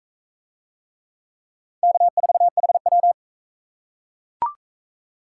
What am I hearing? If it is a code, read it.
Morse code, heard as a keyed tone, about 40 wpm, 700 Hz. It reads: K4HW